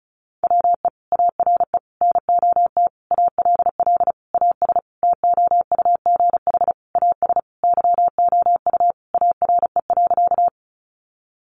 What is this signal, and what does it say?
Morse code, heard as a keyed tone, about 34 wpm, 700 Hz. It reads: WE ARE NOT ALL AS TOUGH AS YOU ARE.
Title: Sentences from Top 1200 Words 35wpm